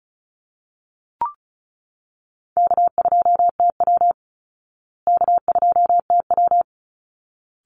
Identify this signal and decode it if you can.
Morse code, heard as a keyed tone, about 34 wpm, 700 Hz. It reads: K2TW K2TW
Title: Easy Call Signs Repeated 2x - 35wpm